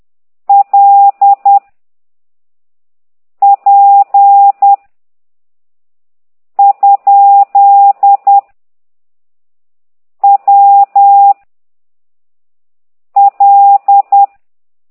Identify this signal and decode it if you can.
Morse code, heard as a keyed tone, about 10 wpm, 800 Hz. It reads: L P ? W L
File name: ct2idw / exerc8